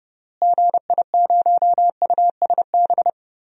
Morse code, heard as a keyed tone, about 30 wpm, 700 Hz. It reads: GI0USB
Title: DXCC-100 Call Signs - 6 Characters - Repeated 3x - 30wpm